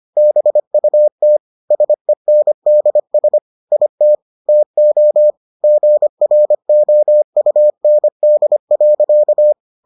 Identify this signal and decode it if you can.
Morse code, heard as a keyed tone, about 25 wpm, 600 Hz. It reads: BUT SENDS IT TO GROUND.